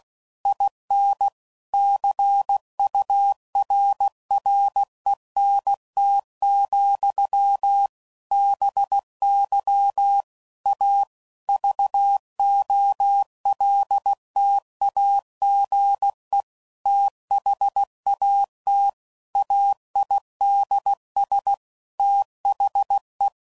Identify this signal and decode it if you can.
Morse code, heard as a keyed tone, about 16 wpm, 800 Hz. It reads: IN CURRENT, BY A VOLTAGE THAT AIDS THE
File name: CW Code Groups @15WPM/Technical Subjects